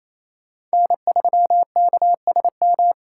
Morse code, heard as a keyed tone, about 35 wpm, 700 Hz. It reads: N3KSM